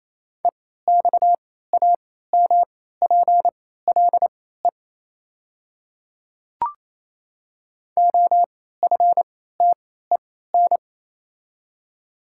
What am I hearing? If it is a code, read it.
Morse code, heard as a keyed tone, about 27 wpm, spaced as for 16 wpm, 700 Hz. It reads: EXAMPLE OFTEN
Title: 3x Character Spacing - Top 300 Words - 28wpm